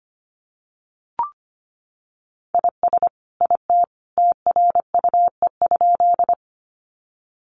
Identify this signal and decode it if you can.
Morse code, heard as a keyed tone, about 25 wpm, 700 Hz. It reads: IS IT TRUE?
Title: Sentences from Top 400 Words 25wpm